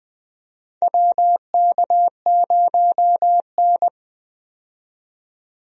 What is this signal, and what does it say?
Morse code, heard as a keyed tone, about 20 wpm, 700 Hz. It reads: WK0N
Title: USA Call Signs - 4 Characters - 20wpm